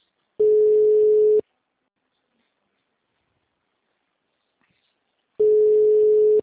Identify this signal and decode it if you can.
Ringback tone (Europe, 425 Hz).